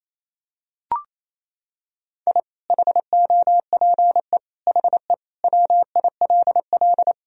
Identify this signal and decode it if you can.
Morse code, heard as a keyed tone, about 28 wpm, 700 Hz. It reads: I HOPE HE WILL
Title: Sentences from Top 1000 Words - Review Part 1 - 28wpm